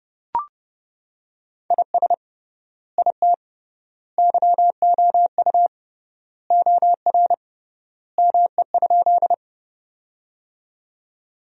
Morse code, heard as a keyed tone, about 30 wpm, 700 Hz. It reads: IS IT YOU OR ME?
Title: 3x Word-Spacing - Sentences from Top 200 Words - 30wpm